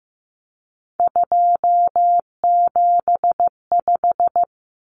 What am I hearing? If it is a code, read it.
Morse code, heard as a keyed tone, about 15 wpm, 700 Hz. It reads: 275
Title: Common Cut Numbers - 3-Digit 15wpm